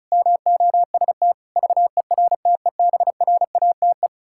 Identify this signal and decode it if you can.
Morse code, heard as a keyed tone, about 36 wpm, 700 Hz. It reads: MOST VERTEBRATE
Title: Pangram Challenge v2 - 35wpm